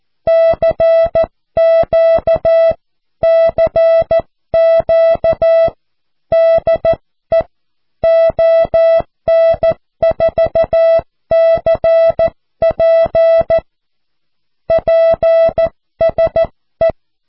Morse code, heard as a keyed tone, about 14 wpm, 660 Hz. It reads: CQ CQ DE ON4CP PSE
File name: cw / QSO